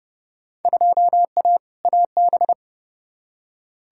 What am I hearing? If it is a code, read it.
Morse code, heard as a keyed tone, about 30 wpm, 700 Hz. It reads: 2A AB
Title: Field Day Exchanges - 30wpm